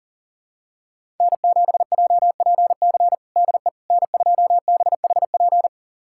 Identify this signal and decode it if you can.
Morse code, heard as a keyed tone, about 37 wpm, 700 Hz. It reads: N7JPC DE N2BHP
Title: USA Callsign DE Callsign - 5 Characters - 40wpm